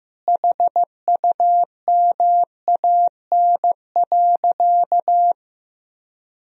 Morse code, heard as a keyed tone, about 16 wpm, 700 Hz. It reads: HUMAN.
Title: Sentences from Top 1000 Words - Review Part 3 - 15wpm